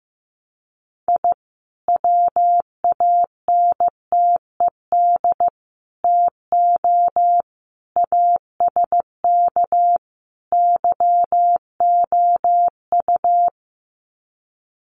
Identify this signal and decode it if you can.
Morse code, heard as a keyed tone, about 15 wpm, 700 Hz. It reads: I WANTED TO ASK YOU
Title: Sets of 5 Words Encore2 15wpm